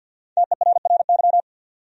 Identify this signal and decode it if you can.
Morse code, heard as a keyed tone, about 49 wpm, 700 Hz. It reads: TERRY
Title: USA Common Names - 50wpm